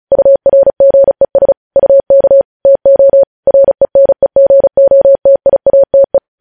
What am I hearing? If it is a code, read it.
Morse code, heard as a keyed tone, about 36 wpm, 550 Hz. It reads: URGES UK TO RENEGOTIATE
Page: news headlines in morse code at 35 wpm